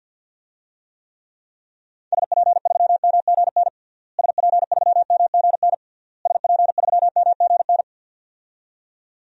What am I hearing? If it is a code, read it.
Morse code, heard as a keyed tone, about 49 wpm, 700 Hz. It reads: SP3MGN SP3MGN SP3MGN
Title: DXCC-100 Call Signs - 6 Characters - Repeated 3x - 50wpm